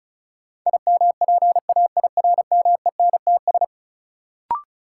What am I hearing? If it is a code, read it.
Morse code, heard as a keyed tone, about 36 wpm, 700 Hz. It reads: IMPAIRMENTS
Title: Sets of 5 Words Encore - New Word Drill - 35wpm